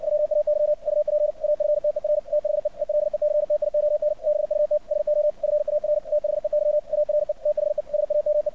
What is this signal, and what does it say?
Morse code, heard as a keyed tone, about 56 wpm, 600 Hz. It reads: WAY UP AFTER THE ELECTION BUT SO FAR THEY ARE THE SAME
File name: archived / CRSnet